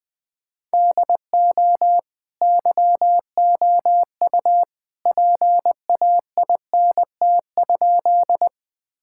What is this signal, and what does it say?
Morse code, heard as a keyed tone, about 20 wpm, 700 Hz. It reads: DO YOU PAINT?
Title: Sentences from Top 1000 Words - Review Part 6 - 20wpm